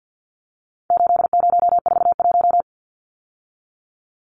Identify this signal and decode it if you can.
Morse code, heard as a keyed tone, about 50 wpm, 700 Hz. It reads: 8041
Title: Numbers - 4-Digit 50wpm